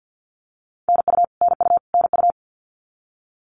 Morse code, heard as a keyed tone, about 52 wpm, 700 Hz. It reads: NU NU NU
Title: Canadian Provinces and Territories - Repeated 3x - 50wpm